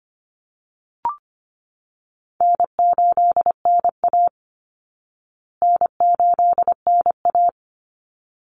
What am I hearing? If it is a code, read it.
Morse code, heard as a keyed tone, about 25 wpm, 700 Hz. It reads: N8NA N8NA
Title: Easy Call Signs Repeated 2x - 25wpm